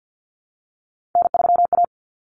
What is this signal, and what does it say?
Morse code, heard as a keyed tone, about 50 wpm, 700 Hz. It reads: N3A
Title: Common Cut Numbers - 3-Digit 50wpm